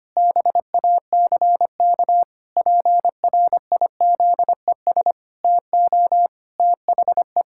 Morse code, heard as a keyed tone, about 25 wpm, 700 Hz. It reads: BACK PRIZES TO THE
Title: Pangram Challenge v2 - 25wpm